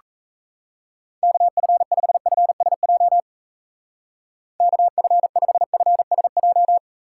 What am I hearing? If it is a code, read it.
Morse code, heard as a keyed tone, about 33 wpm, 700 Hz. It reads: KF5FSJ KF5FSJ
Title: Speed Racing - USA 2x3 Call Signs 28wpm